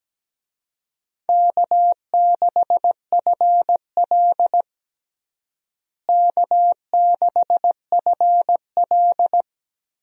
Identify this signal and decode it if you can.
Morse code, heard as a keyed tone, about 17 wpm, 700 Hz. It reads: K6FL K6FL
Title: USA Call Signs - 4 Characters - Repeated 3x - 17wpm